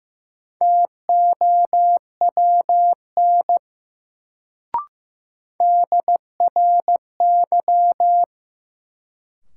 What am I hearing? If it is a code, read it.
Morse code, heard as a keyed tone, about 15 wpm, 700 Hz. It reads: TOWN DRY
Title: Sentences from Top 500 Words - New Word Drill - 15wpm